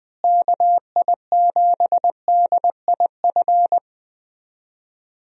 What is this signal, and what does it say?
Morse code, heard as a keyed tone, about 21 wpm, 700 Hz. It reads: KI7DIF
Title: USA Call Signs - 6 Characters - Repeated 3x - 20wpm